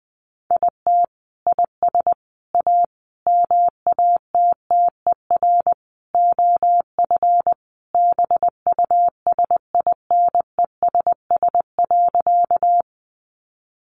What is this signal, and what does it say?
Morse code, heard as a keyed tone, about 20 wpm, 700 Hz. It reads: IT IS A MATTER OF BUSINESS.
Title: Sentences from Top 700 Words 20wpm